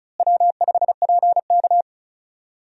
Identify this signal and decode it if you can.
Morse code, heard as a keyed tone, about 34 wpm, 700 Hz. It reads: W5PK
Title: USA Call Signs - 4 Characters - Repeated 3x - 35wpm